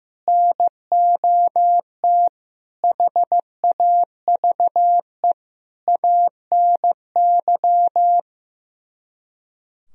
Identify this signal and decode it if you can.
Morse code, heard as a keyed tone, about 15 wpm, 700 Hz. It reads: NOT HAVE ANY
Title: Speed-Racing - Sets of 5 Words 15wpm